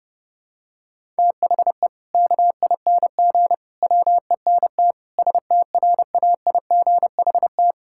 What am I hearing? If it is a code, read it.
Morse code, heard as a keyed tone, about 30 wpm, 700 Hz. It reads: THE KING WENT STRAIGHT
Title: Sentences from Top 1000 Words - Review Part 6 - 30wpm